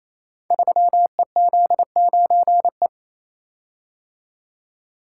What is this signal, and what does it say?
Morse code, heard as a keyed tone, about 28 wpm, 700 Hz. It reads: 3EZ9E
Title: Code Groups - 5 Letter:Numbers - 28wpm